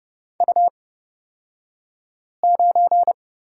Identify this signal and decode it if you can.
Morse code, heard as a keyed tone, about 30 wpm, spaced as for 5 wpm, 700 Hz. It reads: U9
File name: Lesson 37d - Letter x - Two Characters 30wpm